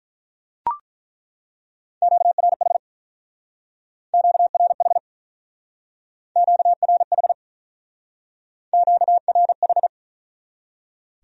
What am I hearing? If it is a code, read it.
Morse code, heard as a keyed tone, about 43 wpm, 700 Hz. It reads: QRH QRH QRH QRH
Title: Speed-Racing Q-Codes - 35wpm